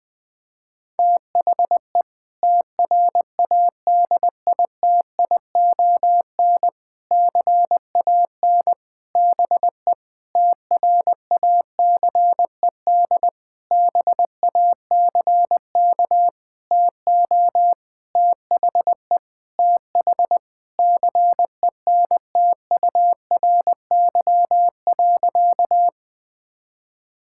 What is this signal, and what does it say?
Morse code, heard as a keyed tone, about 20 wpm, 700 Hz. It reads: THE TRADITION CAN BE TRACED BACK TO THE TH CENTURY.
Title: Sentences with 6-Word Phrases - Spoken Beginning - 20wpm